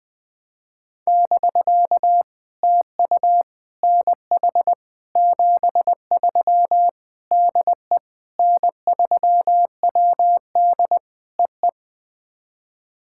Morse code, heard as a keyed tone, about 20 wpm, 700 Hz. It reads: <BK> TU NH 73 DE N3WD EE
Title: POTA QSO Practice - 20wpm